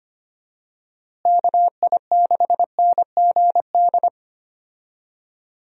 Morse code, heard as a keyed tone, about 25 wpm, 700 Hz. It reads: KI6NGD
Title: USA Call Signs - 6 Characters - Repeated 3x - 25wpm